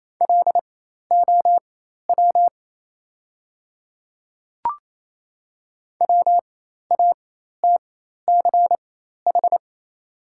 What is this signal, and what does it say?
Morse code, heard as a keyed tone, about 28 wpm, spaced as for 13 wpm, 690 Hz. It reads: LOW WATCH
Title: 4x Character Spacing - Top 300 Words - 28wpm